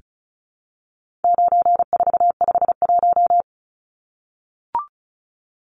Morse code, heard as a keyed tone, about 34 wpm, 700 Hz. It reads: 9451
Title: Numbers - 4-Digit 35wpm